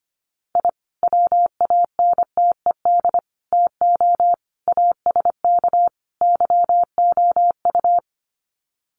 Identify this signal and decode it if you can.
Morse code, heard as a keyed tone, about 25 wpm, 700 Hz. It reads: I WANTED TO ASK YOU
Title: Sets of 5 Words Encore2 25wpm